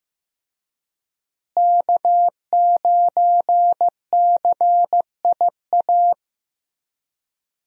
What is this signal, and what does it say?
Morse code, heard as a keyed tone, about 15 wpm, 700 Hz. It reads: K9CIA
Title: USA Call Signs - 5 Characters - Repeated 3x - 15wpm